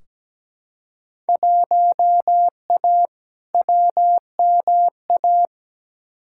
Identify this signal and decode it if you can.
Morse code, heard as a keyed tone, about 17 wpm, 700 Hz. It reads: 1A WMA
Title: Field Day Exchanges - 17wpm